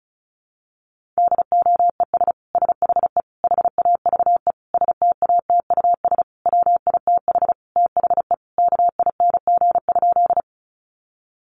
Morse code, heard as a keyed tone, about 34 wpm, 700 Hz. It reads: DOES SHE HAVE STATUS WITH THE KING?
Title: Sentences from Top 1400 Words 35wpm